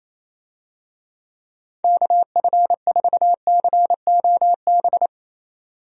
Morse code, heard as a keyed tone, about 28 wpm, 700 Hz. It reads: KF4COB
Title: USA Call Signs - 6 Characters - 28wpm